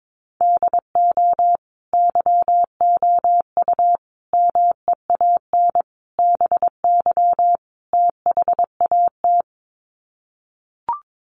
Morse code, heard as a keyed tone, about 22 wpm, 700 Hz. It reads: DO YOU MEAN BY THAT
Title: Sets of 5 Words Encore2 22wpm